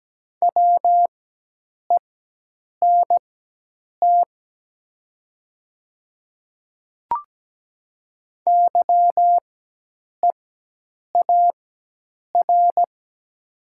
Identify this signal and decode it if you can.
Morse code, heard as a keyed tone, about 17 wpm, spaced as for 8 wpm, 700 Hz. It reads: WENT YEAR
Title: 4x Character Spacing - Top 200 Words - 17wpm